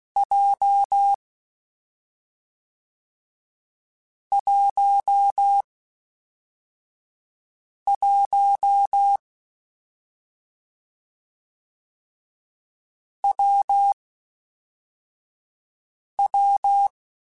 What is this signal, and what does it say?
Morse code, heard as a keyed tone, about 16 wpm, 800 Hz. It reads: J 1 1 W W
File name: Difficult Combos / Difficult Combos-mp3s / WPJ WPJ1 @5